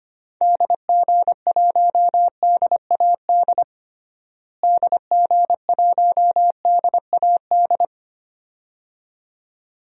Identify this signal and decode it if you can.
Morse code, heard as a keyed tone, about 25 wpm, 700 Hz. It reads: DG1DAD DG1DAD
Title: DXCC-100 Call Signs - 6 Characters - Repeated 2x - 25wpm